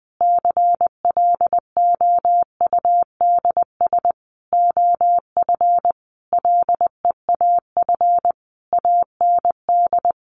Morse code, heard as a keyed tone, about 20 wpm, 700 Hz. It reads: CLOUDS OF LEAF AND
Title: A Sound Of Thunder 20wpm